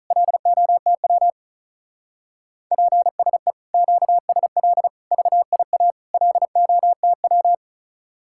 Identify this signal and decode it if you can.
Morse code, heard as a keyed tone, about 45 wpm, 700 Hz. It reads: LOTW PSE QSL VIA LOTW